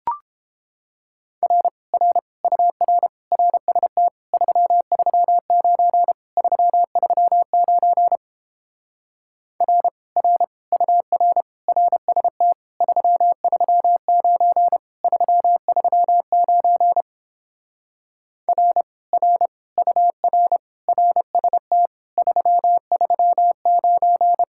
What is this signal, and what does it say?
Morse code, heard as a keyed tone, about 29 wpm, 700 Hz. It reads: R R UR RST 339 339 R R UR RST 339 339 R R UR RST 339